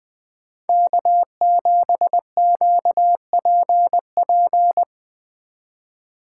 Morse code, heard as a keyed tone, about 20 wpm, 700 Hz. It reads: K7QPP
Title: Speed Racing - USA 1x3 Call Signs 17wpm